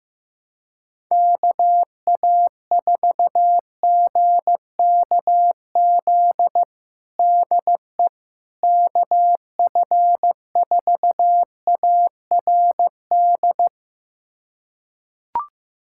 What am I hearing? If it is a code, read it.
Morse code, heard as a keyed tone, about 15 wpm, 700 Hz. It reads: KA4GKZ DE KF4ARD
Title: USA Callsign DE Callsign - 6 Characters - 15wpm